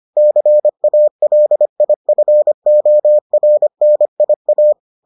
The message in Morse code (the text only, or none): CALIFORNIA